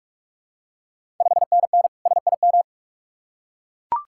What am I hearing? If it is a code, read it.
Morse code, heard as a keyed tone, about 44 wpm, 690 Hz. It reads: 5NN SIM
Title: Ontario QSO Party Exchange 45wpm